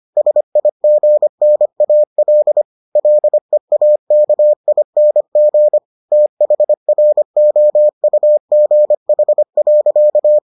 Morse code, heard as a keyed tone, about 25 wpm, 600 Hz. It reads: SIGNAL LEAKING THROUGH.